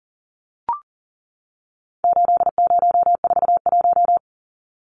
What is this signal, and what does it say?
Morse code, heard as a keyed tone, about 40 wpm, 700 Hz. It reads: 8041
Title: Numbers - 4-Digit 40wpm